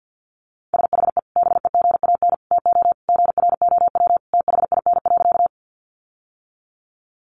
Morse code, heard as a keyed tone, about 50 wpm, 700 Hz. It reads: SHE BEGAN TO GROW THIN.
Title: Sentences from Top 1200 Words 50wpm